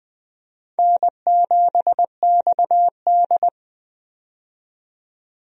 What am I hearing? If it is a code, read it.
Morse code, heard as a keyed tone, about 20 wpm, 700 Hz. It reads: N7XD
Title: USA Call Signs - 4 Characters - 20wpm